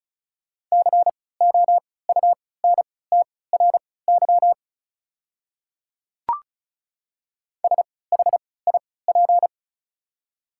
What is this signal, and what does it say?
Morse code, heard as a keyed tone, about 34 wpm, spaced as for 20 wpm, 690 Hz. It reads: COUNTRY SHIP
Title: 3x Character Spacing - Top 500 Words - 35wpm